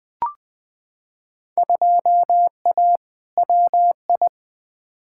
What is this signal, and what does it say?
Morse code, heard as a keyed tone, about 20 wpm, 700 Hz. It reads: 2A WI
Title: Field Day Exchanges Encore - 20wpm